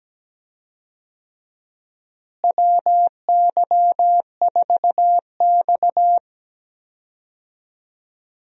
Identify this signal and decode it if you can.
Morse code, heard as a keyed tone, about 17 wpm, 700 Hz. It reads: WY4X
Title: USA Call Signs - 4 Characters - Repeated 2x - 17wpm